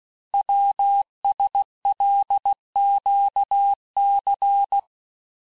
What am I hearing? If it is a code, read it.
Morse code, heard as a keyed tone, about 16 wpm, 800 Hz. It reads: WSLQC
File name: cours-du-lundi files / cours-1592--teste-chiffre-vitesse-15mpm